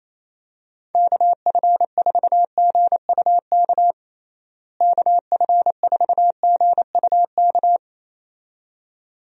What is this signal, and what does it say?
Morse code, heard as a keyed tone, about 28 wpm, 700 Hz. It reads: KF4GUK KF4GUK
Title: USA Call Signs - 6 Characters - Repeated 2x - 28wpm